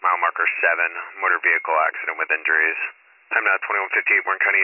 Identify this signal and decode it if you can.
Emergency Alert System (SAME) tones, heard as harsh buzzing bursts, with speech. It is heard from 0.4 to 4.3 s.